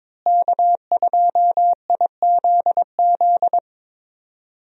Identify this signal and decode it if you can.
Morse code, heard as a keyed tone, about 22 wpm, 700 Hz. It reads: K2IZZ
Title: USA Call Signs - 5 Characters - Repeated 2x - 22wpm